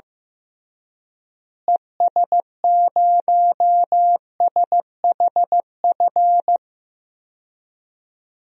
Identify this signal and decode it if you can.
Morse code, heard as a keyed tone, about 15 wpm, 700 Hz. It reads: ES0SHF
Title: DXCC-100 Call Signs - 6 Characters - Repeated 3x - 15wpm